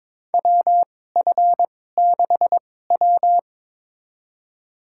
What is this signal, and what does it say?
Morse code, heard as a keyed tone, about 22 wpm, spaced as for 16 wpm, 700 Hz. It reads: WF6W
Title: USA Call Signs - 4 Characters - 2x Character Spacing - 22wpm